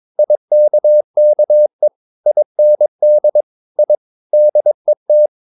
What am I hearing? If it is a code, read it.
Morse code, heard as a keyed tone, about 22 wpm, 600 Hz. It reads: IKKE IND I DET
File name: Ferie QRS 22wpm